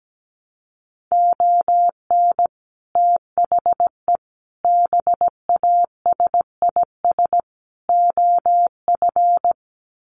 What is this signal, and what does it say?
Morse code, heard as a keyed tone, about 17 wpm, 700 Hz. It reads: ON THE BASIS OF
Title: Sets of 4 Words 17wpm